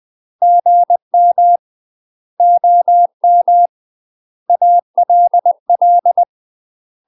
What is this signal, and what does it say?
Morse code, heard as a keyed tone, about 20 wpm, 700 Hz. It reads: GM OM ALL
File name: UK MASTER QSOs TRAINING / 20WPM / UK MASTER QSOs 0018